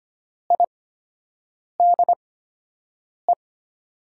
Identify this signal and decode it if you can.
Morse code, heard as a keyed tone, about 25 wpm, spaced as for 7 wpm, 700 Hz. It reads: IDE
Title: 8x Character Spacing - Top 100 Words - 25wpm